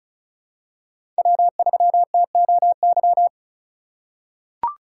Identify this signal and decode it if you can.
Morse code, heard as a keyed tone, about 36 wpm, 700 Hz. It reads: W3TOY
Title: USA Call Signs - 8x Character Spacing - 35wpm